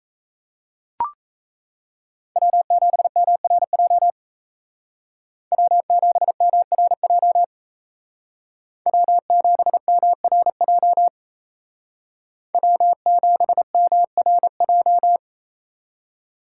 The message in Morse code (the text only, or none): W7MRJ W7MRJ W7MRJ W7MRJ